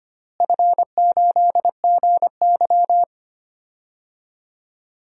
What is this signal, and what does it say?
Morse code, heard as a keyed tone, about 25 wpm, 700 Hz. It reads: F8GY
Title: DXCC-100 Call Signs - 4 Characters - Repeated 2x - 25wpm